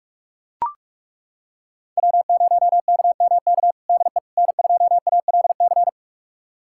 Morse code, heard as a keyed tone, about 41 wpm, 700 Hz. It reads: W0KMK DE N2ALC
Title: USA Callsign DE Callsign - 5 Characters - 45wpm